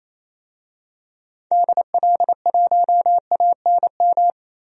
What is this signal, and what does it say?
Morse code, heard as a keyed tone, about 28 wpm, 700 Hz. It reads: DL1ANM
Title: DXCC-100 Call Signs - 6 Characters - 28wpm